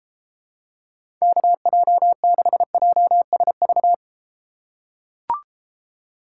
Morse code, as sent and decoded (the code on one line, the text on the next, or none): -.- .--- -.... .--- ... ...-
KJ6JSV